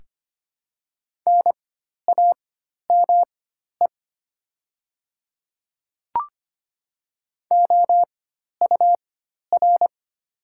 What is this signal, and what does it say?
Morse code, heard as a keyed tone, about 25 wpm, spaced as for 12 wpm, 700 Hz. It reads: NAME OUR
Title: 4x Character Spacing - Top 500 Words - 25wpm